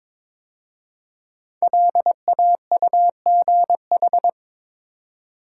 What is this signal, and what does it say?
Morse code, heard as a keyed tone, about 22 wpm, 700 Hz. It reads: LAUGH
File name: 5 Letter Words - First 3 Letters Given - 22wpm